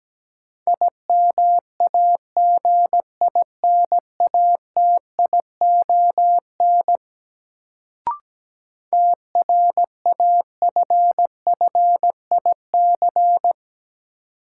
Morse code, heard as a keyed tone, about 17 wpm, 700 Hz. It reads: IMAGINATION TRAFFIC
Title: Sentences from Top 1300 Words - New Word Drill Alt - 17wpm